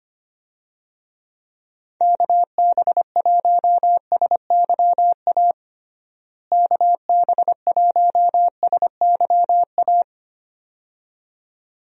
Morse code, heard as a keyed tone, about 25 wpm, 700 Hz. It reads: KB1SYA KB1SYA
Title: USA Call Signs - 6 Characters - Repeated 2x - 25wpm